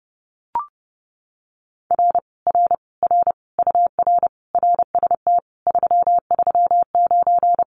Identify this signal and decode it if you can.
Morse code, heard as a keyed tone, about 30 wpm, 700 Hz. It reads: R R R UR RST 339
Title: Common QSO Phrases Encore 30wpm